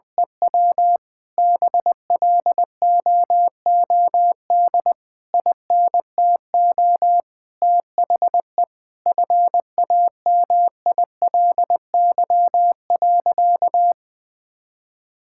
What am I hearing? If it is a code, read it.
Morse code, heard as a keyed tone, about 21 wpm, 700 Hz. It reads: EW BLOOD INTO THE FAMILY.
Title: Sentences from Top 1600 Words 20wpm